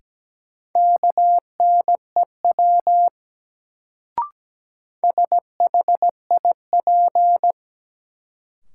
Morse code, heard as a keyed tone, about 17 wpm, 700 Hz. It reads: KNEW SHIP
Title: Sentences from Top 400 Words - New Word Drill - 17wpm